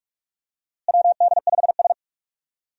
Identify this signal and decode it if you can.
Morse code, heard as a keyed tone, about 44 wpm, 700 Hz. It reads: WD5S
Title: USA Call Signs - 4 Characters - Repeated 2x - 45wpm